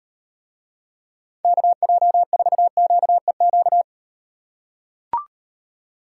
Morse code, heard as a keyed tone, about 38 wpm, 700 Hz. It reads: KJ4QEQ